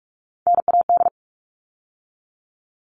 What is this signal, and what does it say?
Morse code, heard as a keyed tone, about 44 wpm, 700 Hz. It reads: NAD
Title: All Cut Numbers - 3-Digit 45wpm